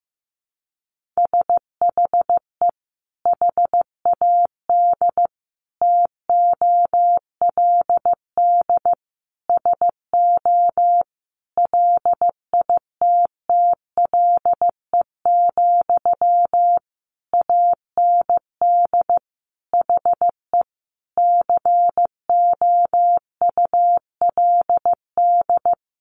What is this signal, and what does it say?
Morse code, heard as a keyed tone, about 15 wpm, 700 Hz. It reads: SHE HAD TOLD SO LITTLE, AND HE COULD
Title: Sentences from Top 1500 Words 15wpm